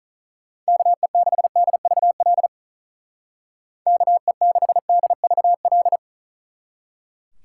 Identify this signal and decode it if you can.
Morse code, heard as a keyed tone, about 45 wpm, 700 Hz. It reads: KE6DVL KE6DVL